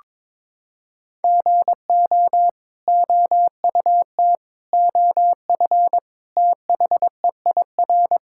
Morse code, heard as a keyed tone, about 22 wpm, 700 Hz. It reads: GO OUT OF THEIR
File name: practice / 22wpm / Sets of 6 Words - 22wpm